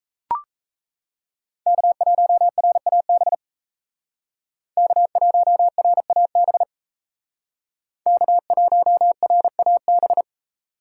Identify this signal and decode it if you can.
Morse code, heard as a keyed tone, about 36 wpm, 700 Hz. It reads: K1RAB K1RAB K1RAB